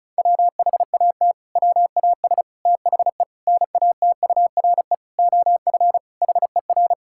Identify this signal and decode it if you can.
Morse code, heard as a keyed tone, about 36 wpm, 700 Hz. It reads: WHAT WAS THE NATURE OF HER
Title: Sentences from Top 1000 Words - Review Part 6 - 35wpm